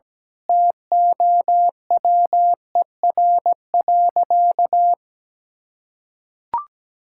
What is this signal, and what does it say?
Morse code, heard as a keyed tone, about 17 wpm, 700 Hz. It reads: TOWER.
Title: Pangram Challenge v2 - 17wpm